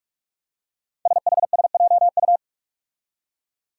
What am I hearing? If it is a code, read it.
Morse code, heard as a keyed tone, about 46 wpm, 700 Hz. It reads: SHSJU
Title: Five Challenging Letter-Number Combinations 45wpm